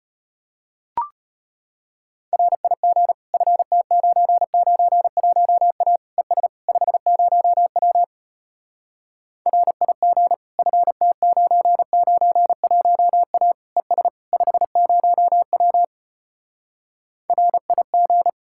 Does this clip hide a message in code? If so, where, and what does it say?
Yes: Morse code from 2.3 to 18.3 s, heard as a keyed tone, about 33 wpm, 700 Hz. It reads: RIG FT991A ES 50W RIG FT991A ES 50W RIG